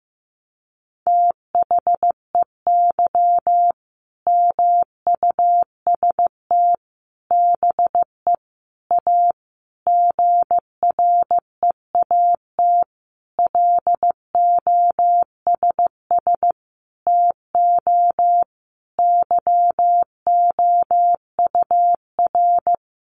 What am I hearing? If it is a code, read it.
Morse code, heard as a keyed tone, about 15 wpm, 700 Hz. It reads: THEY MUST BE A GREAT LOSS TO YOUR
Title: Sentences from Top 1400 Words 15wpm